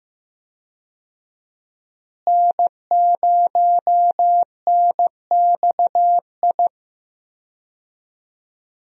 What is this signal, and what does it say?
Morse code, heard as a keyed tone, about 15 wpm, 700 Hz. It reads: N0NXI
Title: USA Call Signs - 5 Characters - Repeated 3x - 15wpm